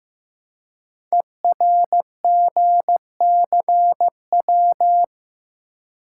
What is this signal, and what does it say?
Morse code, heard as a keyed tone, about 15 wpm, 700 Hz. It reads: ERGCW
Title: Code Groups - 5 Letter:Numbers - 15wpm